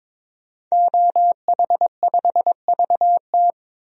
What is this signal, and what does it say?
Morse code, heard as a keyed tone, about 21 wpm, 700 Hz. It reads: OH5VT
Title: DXCC-100 Call Signs - 5 Characters - 22wpm